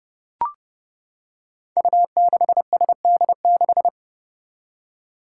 Morse code, heard as a keyed tone, about 30 wpm, 700 Hz. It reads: U6SD6
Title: Five Challenging Letter-Number Combinations 30wpm